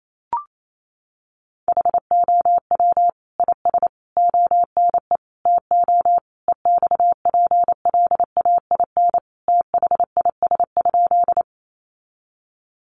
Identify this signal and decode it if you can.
Morse code, heard as a keyed tone, about 28 wpm, 700 Hz. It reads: HOW IS ONE TO EXPLAIN THIS?
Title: Sentences from Top 1100 Words 28wpm